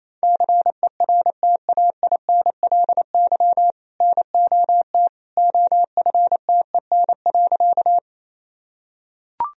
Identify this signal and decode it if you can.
Morse code, heard as a keyed tone, about 27 wpm, 700 Hz. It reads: CERTAINLY NOT OFTEN.
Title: Sentences from Top 1000 Words - Review Part 1 - 28wpm